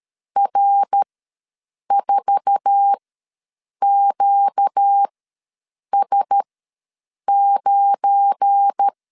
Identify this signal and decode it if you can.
Morse code, heard as a keyed tone, about 13 wpm, spaced as for 7 wpm, 800 Hz. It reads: R4QS9
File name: LEZIONI / 8 YJ45 / 8 YJ4 ws 12wpm 3249